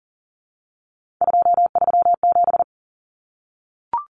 Morse code, heard as a keyed tone, about 40 wpm, 700 Hz. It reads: 237
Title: Numbers - 3-Digit 40wpm